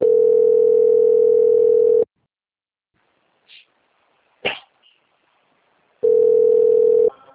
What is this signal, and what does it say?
Ringback tone (North America), 440 + 480 Hz.